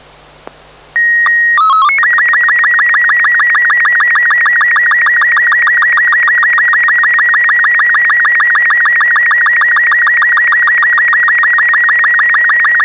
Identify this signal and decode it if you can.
SSTV (slow-scan television), Robot 36.